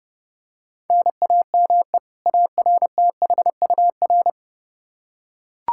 Morse code, heard as a keyed tone, about 32 wpm, 700 Hz. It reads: NAME ARTHUR
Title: Practice QSO - Exchange 1 - Beginning - 30wpm